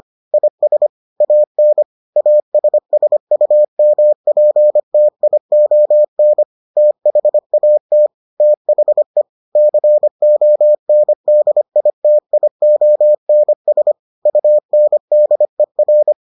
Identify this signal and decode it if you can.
Morse code, heard as a keyed tone, about 25 wpm, 600 Hz. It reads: IS AN ASSUMPTION THAT THE CONDITIONS UNDER